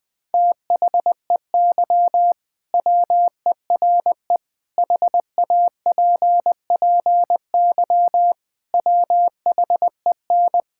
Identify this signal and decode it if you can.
Morse code, heard as a keyed tone, about 21 wpm, 700 Hz. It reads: THEY WERE HAPPY WHEN
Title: Sentences with 2-Word Phrases - Spoken Beginning - 20wpm